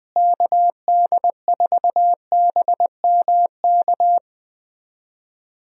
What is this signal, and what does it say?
Morse code, heard as a keyed tone, about 21 wpm, 700 Hz. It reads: KD4BMK
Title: USA Call Signs - 6 Characters - Repeated 3x - 20wpm